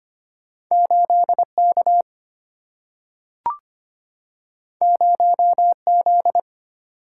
Morse code, heard as a keyed tone, about 25 wpm, 700 Hz. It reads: 8 K 0 Z
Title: Single-Letter Single-Number 25wpm